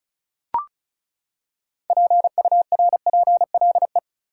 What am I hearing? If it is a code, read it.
Morse code, heard as a keyed tone, about 36 wpm, 700 Hz. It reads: PURPLE
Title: Six Letter Words - 35wpm